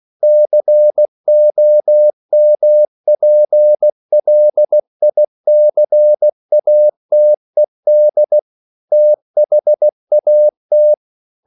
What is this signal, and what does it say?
Morse code, heard as a keyed tone, about 16 wpm, 600 Hz. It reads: COMPLICATED THAT
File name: CW / Practice / EnglishQuotations0216